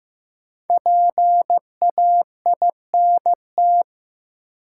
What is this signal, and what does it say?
Morse code, heard as a keyed tone, about 15 wpm, 700 Hz. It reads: PAINT
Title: Sentences from Top 1000 Words - Review Part 6 - New Word Drill - 15wpm